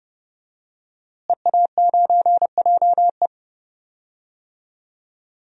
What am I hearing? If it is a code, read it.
Morse code, heard as a keyed tone, about 30 wpm, 700 Hz. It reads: EA9JE
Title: DXCC-100 Call Signs - 5 Characters - Repeated 2x - 30wpm